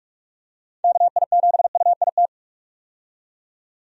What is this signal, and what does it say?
Morse code, heard as a keyed tone, about 47 wpm, 700 Hz. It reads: KI7UIT
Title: USA Call Signs - 6 Characters - 45wpm